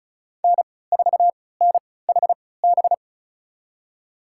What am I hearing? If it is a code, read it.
Morse code, heard as a keyed tone, about 36 wpm, spaced as for 20 wpm, 700 Hz. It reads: N4NHB
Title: USA Call Signs - 5 Characters - 3x Character Spacing - 35wpm